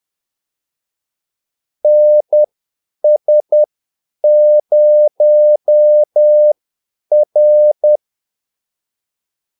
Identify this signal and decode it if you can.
Morse code, heard as a keyed tone, about 10 wpm, spaced as for 8 wpm, 600 Hz. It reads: NS0R
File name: Callsigns-10-8wpm